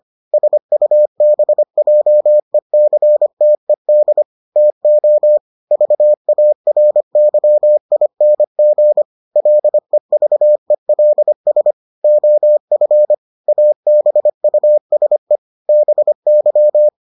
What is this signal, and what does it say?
Morse code, heard as a keyed tone, about 25 wpm, 600 Hz. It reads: SUBJECTED TO VARYING LEVELS OF ABUSE BY